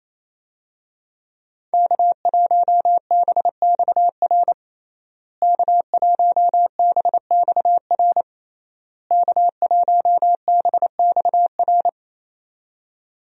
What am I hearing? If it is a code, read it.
Morse code, heard as a keyed tone, about 28 wpm, 700 Hz. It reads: K1BXR K1BXR K1BXR
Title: USA Call Signs - 5 Characters - Repeated 3x - 28wpm